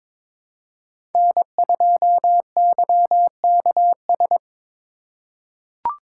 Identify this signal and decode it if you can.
Morse code, heard as a keyed tone, about 21 wpm, 700 Hz. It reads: N2YKS
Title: USA Call Signs - 5 Characters - 2x Character Spacing - 22wpm